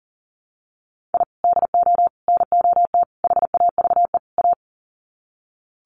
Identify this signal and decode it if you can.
Morse code, heard as a keyed tone, about 40 wpm, 700 Hz. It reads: I DO NOT HAVE A
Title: Sets of 5 Words 40wpm